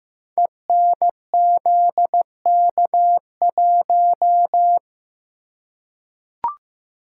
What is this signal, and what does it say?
Morse code, heard as a keyed tone, about 15 wpm, 700 Hz. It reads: ENZK1
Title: Code Groups - 5 Letter:Numbers - 15wpm